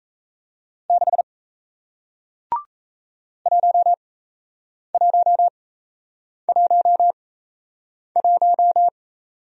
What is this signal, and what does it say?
Morse code, heard as a keyed tone, about 38 wpm, spaced as for 8 wpm, 700 Hz. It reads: 6 1111